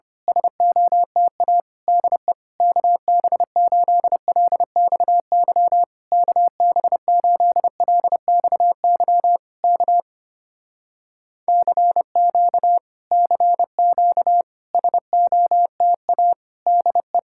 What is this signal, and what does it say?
Morse code, heard as a keyed tone, about 26 wpm, 700 Hz. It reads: SOTA DE KB8LXY KB8LXY K CQ CQ SOTA DE
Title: Speed-Racing Common QSO Phrases - 25wpm